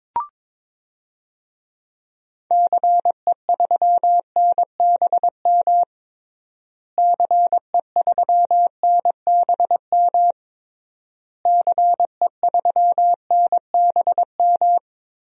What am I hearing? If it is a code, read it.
Morse code, heard as a keyed tone, about 22 wpm, 700 Hz. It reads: CE3NBM CE3NBM CE3NBM
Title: DXCC-100 Call Signs - 6 Characters - Repeated 3x - 22wpm